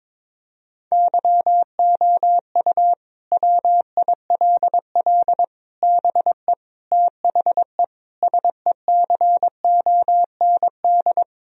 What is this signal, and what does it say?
Morse code, heard as a keyed tone, about 22 wpm, 700 Hz. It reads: YOU WILL BE THE SECOND
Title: Sentences from Top 1000 Words - Review Part 4 - 22wpm